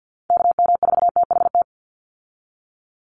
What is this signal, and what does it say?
Morse code, heard as a keyed tone, about 48 wpm, 700 Hz. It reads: KM4THT
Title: USA Call Signs 50wpm